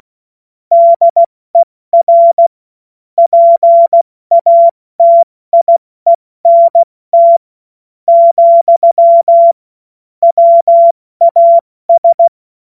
Morse code, heard as a keyed tone, about 16 wpm, 680 Hz. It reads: DER PATIENT , WAS
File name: Morsekurs / Witze / Witz 11. Juli 2023 70 F80